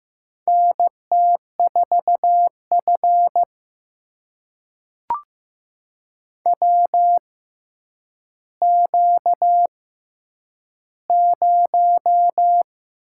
Morse code, heard as a keyed tone, about 15 wpm, spaced as for 9 wpm, 700 Hz. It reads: NT4F WQ0
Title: USA Call Signs - 4 Characters - 6x Character Spacing - 15wpm